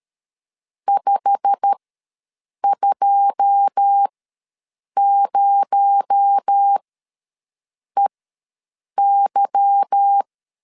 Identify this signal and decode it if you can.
Morse code, heard as a keyed tone, about 13 wpm, spaced as for 7 wpm, 800 Hz. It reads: 520EY
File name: LEZIONI / 9 2 ws 12wpm 3406